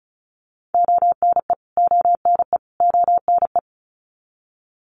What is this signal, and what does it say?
Morse code, heard as a keyed tone, about 34 wpm, 700 Hz. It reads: ONE ONE ONE
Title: Top 200 Words Repeated 35wpm